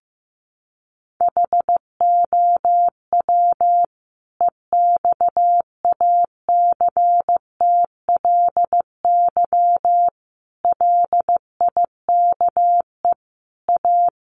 Morse code, heard as a keyed tone, about 15 wpm, 700 Hz. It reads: HOW EXACTLY LIKE A
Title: Sentences from Top 900 Words 15wpm